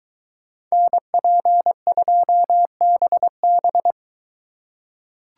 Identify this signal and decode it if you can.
Morse code, heard as a keyed tone, about 21 wpm, 700 Hz. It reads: NP2BB